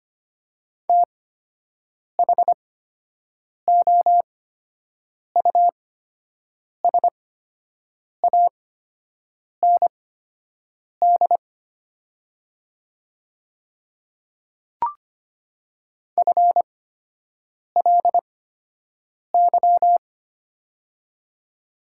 Morse code, heard as a keyed tone, about 25 wpm, spaced as for 7 wpm, 700 Hz. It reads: THOUSAND FLY
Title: 8x Character Spacing - Top 500 Words - 25wpm